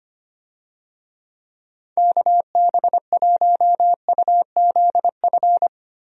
Morse code, heard as a keyed tone, about 25 wpm, 700 Hz. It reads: KB1UZF
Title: USA Call Signs - 6 Characters - Repeated 2x - 25wpm